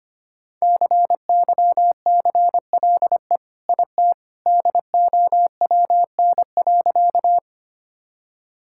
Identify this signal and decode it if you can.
Morse code, heard as a keyed tone, about 25 wpm, 700 Hz. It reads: CYCLE IT DOWN.
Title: Sentences from Top 1000 Words - Review Part 6 - 25wpm